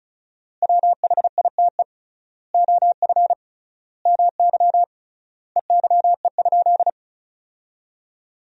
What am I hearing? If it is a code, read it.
Morse code, heard as a keyed tone, about 36 wpm, 700 Hz. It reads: WHITE OF MY EYE?
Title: 3x Word-Spacing - Sentences from Top 500 Words - 35wpm